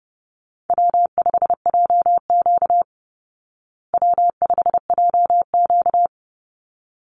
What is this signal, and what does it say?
Morse code, heard as a keyed tone, about 30 wpm, 700 Hz. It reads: W5JQ W5JQ
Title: Call Signs Repeated 3x - 30wpm